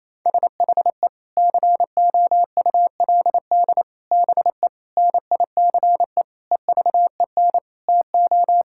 Morse code, heard as a keyed tone, about 27 wpm, 700 Hz. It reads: SHE COULD BE NICE EVEN TO
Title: Sentences from Top 1600 Words 28wpm